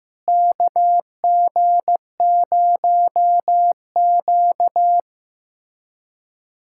Morse code, heard as a keyed tone, about 15 wpm, 700 Hz. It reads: KG0Q
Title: USA Callsign DE Callsign - 4 Characters - 15wpm